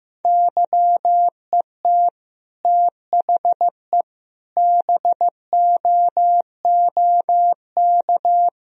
Morse code, heard as a keyed tone, about 15 wpm, 700 Hz. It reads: YET THE BOOK
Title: Sentences from Top 1000 Words - Review Part 4 - 15wpm